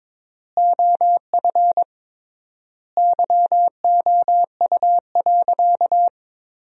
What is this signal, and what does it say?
Morse code, heard as a keyed tone, about 22 wpm, 700 Hz. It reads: OF YOU.
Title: 3x Word-Spacing - Sentences from Top 200 Words - 22wpm